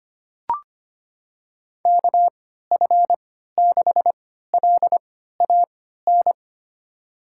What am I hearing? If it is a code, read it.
Morse code, heard as a keyed tone, about 25 wpm, spaced as for 14 wpm, 700 Hz. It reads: KF6LAN
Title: USA Call Signs - 6 Characters - 3x Character Spacing - 25wpm